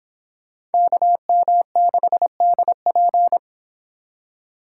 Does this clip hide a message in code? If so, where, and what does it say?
Yes: Morse code from 0.7 to 3.4 s, heard as a keyed tone, about 30 wpm, 700 Hz. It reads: KM6DP